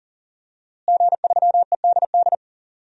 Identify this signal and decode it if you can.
Morse code, heard as a keyed tone, about 40 wpm, 700 Hz. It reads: G3EDD
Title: DXCC-100 Call Signs - 5 Characters - Repeated 3x - 40wpm